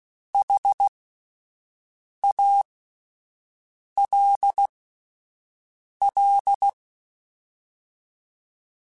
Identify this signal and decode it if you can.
Morse code, heard as a keyed tone, about 16 wpm, spaced as for 5 wpm, 800 Hz. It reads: HALL